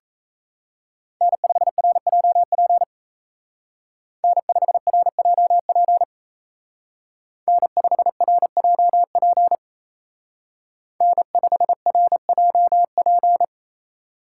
Morse code, heard as a keyed tone, about 35 wpm, 700 Hz. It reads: N5RJP N5RJP N5RJP N5RJP